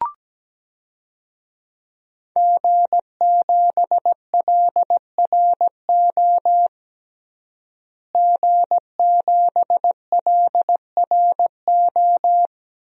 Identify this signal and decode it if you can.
Morse code, heard as a keyed tone, about 17 wpm, 700 Hz. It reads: G7LRO G7LRO
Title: DXCC-100 Call Signs - 5 Characters - Repeated 3x - 17wpm